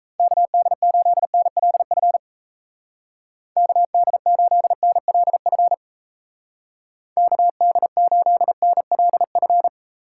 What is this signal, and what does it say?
Morse code, heard as a keyed tone, about 34 wpm, 700 Hz. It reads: KD8NLF KD8NLF KD8NLF
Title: Speed Racing - USA 2x3 Call Signs 28wpm